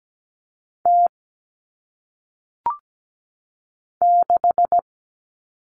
Morse code, heard as a keyed tone, about 17 wpm, 700 Hz. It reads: T 6
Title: All Cut Numbers - 1-Digit 17wpm